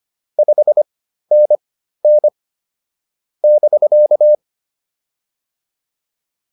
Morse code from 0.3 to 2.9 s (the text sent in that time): 5NN